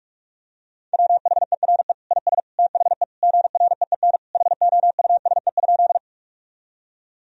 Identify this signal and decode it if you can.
Morse code, heard as a keyed tone, about 46 wpm, 700 Hz. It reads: WHERE IS THE GREEN HOUSE?
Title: Sentences from Top 1000 Words - Review Part 2 - 45wpm